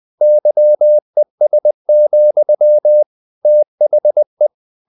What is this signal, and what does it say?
Morse code, heard as a keyed tone, about 20 wpm, 600 Hz. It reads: YES, THE
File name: the old man and the sea / the old man and the sea part 1 20wpm